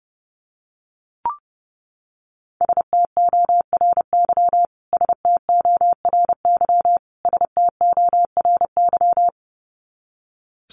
Morse code, heard as a keyed tone, about 30 wpm, 700 Hz. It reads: STORY STORY STORY
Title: Top 500 Words Repeated 30wpm